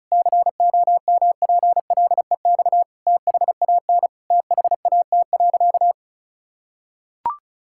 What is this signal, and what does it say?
Morse code, heard as a keyed tone, about 34 wpm, 700 Hz. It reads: COMPLEX THAN THAT.
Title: Sentences from Top 1000 Words - Review Part 6 - 35wpm